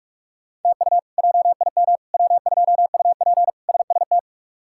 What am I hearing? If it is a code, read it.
Morse code, heard as a keyed tone, about 47 wpm, 700 Hz. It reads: TU JIM W2UP SST
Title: K1USN SST QSO Practice - 45wpm